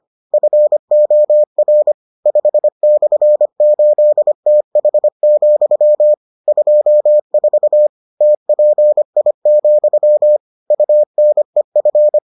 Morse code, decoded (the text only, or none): FOR 5/8TH, 24 TPI, UNEF